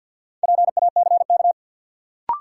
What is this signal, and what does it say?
Morse code, heard as a keyed tone, about 48 wpm, 700 Hz. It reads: PACK
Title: Sentences from Top 1300 Words - New Word Drill Alt - 50wpm